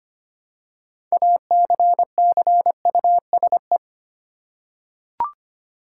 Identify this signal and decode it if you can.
Morse code, heard as a keyed tone, about 25 wpm, 700 Hz. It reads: ACCUSE
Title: Six Letter Words - 25wpm